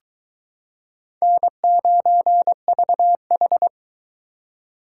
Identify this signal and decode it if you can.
Morse code, heard as a keyed tone, about 22 wpm, 700 Hz. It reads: N9VH